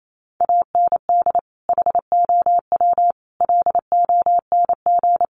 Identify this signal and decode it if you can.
Morse code, heard as a keyed tone, about 28 wpm, 700 Hz. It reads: AND HOW LONG
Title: Sentences from Top 1200 Words 28wpm